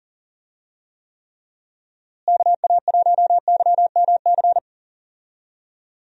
Morse code, heard as a keyed tone, about 40 wpm, 700 Hz. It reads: KA1YMC
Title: USA Call Signs - 6 Characters - Repeated 3x - 40wpm